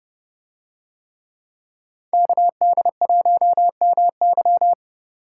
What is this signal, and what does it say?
Morse code, heard as a keyed tone, about 28 wpm, 700 Hz. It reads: KD1MY